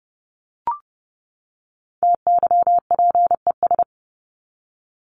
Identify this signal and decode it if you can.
Morse code, heard as a keyed tone, about 30 wpm, 700 Hz. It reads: TYPES
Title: ES:S Suffix - Top 100 Words with Base Words 30wpm